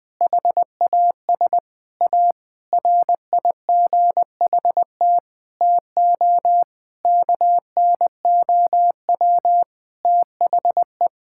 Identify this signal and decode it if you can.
Morse code, heard as a keyed tone, about 20 wpm, 700 Hz. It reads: HAS A RIGHT TO KNOW THE
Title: Sentences with 7-Word Phrases - 20wpm